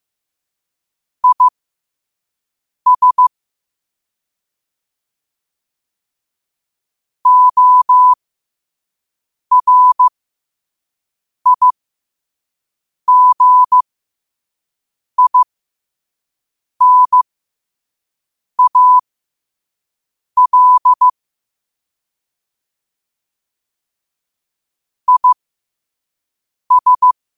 Morse code, heard as a keyed tone, about 15 wpm, spaced as for 5 wpm, 1000 Hz. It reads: IS ORIGINAL IS